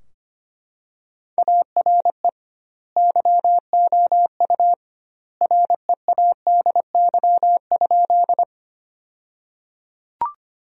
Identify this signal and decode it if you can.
Morse code, heard as a keyed tone, about 25 wpm, 700 Hz. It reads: ARE YOU READY?
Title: 2x Word-Spacing - Sentences from Top 400 Words - 25wpm